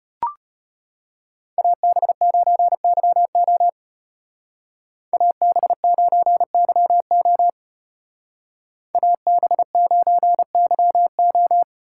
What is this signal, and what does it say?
Morse code, heard as a keyed tone, about 33 wpm, 700 Hz. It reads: AB9YO AB9YO AB9YO